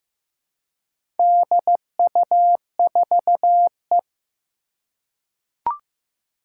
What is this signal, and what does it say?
Morse code, heard as a keyed tone, about 15 wpm, 700 Hz. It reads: DU4E
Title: Four Challenging Letter-Number Combinations 15wpm